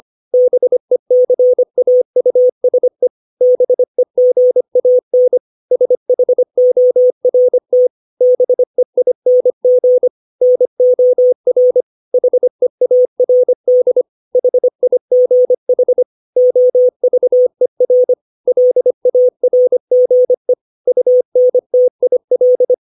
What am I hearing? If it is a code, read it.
Morse code, heard as a keyed tone, about 25 wpm, 500 Hz. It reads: BECAUSE BEGAN SHORT BEING NOR HEARD HIGH OVER LARGE UNTIL
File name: 632 words / 632 Words 25WPM